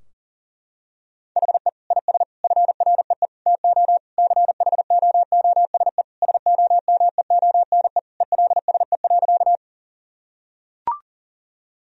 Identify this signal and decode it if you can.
Morse code, heard as a keyed tone, about 40 wpm, 700 Hz. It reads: HE IS FREE TO CHOOSE SOMEONE ELSE.
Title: Sentences from Top 1000 Words - Review Part 4 - 40wpm